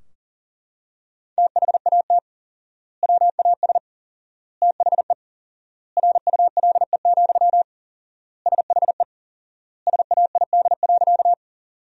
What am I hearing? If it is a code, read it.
Morse code, heard as a keyed tone, about 40 wpm, 700 Hz. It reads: THAT WAS THE RULE, SHE SAID.
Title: 4x Word-Spacing - Sentences from Top 400 Words - 40wpm